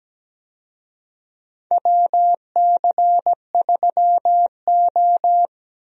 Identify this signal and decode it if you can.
Morse code, heard as a keyed tone, about 17 wpm, 700 Hz. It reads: WC3O
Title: USA Call Signs - 4 Characters - Repeated 2x - 17wpm